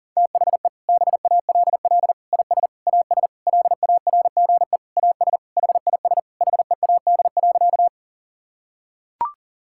Morse code, heard as a keyed tone, about 40 wpm, 700 Hz. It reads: THE BALL IS AS LARGE AS HIS HEAD.
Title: Sentences from Top 1000 Words - Review Part 6 - 40wpm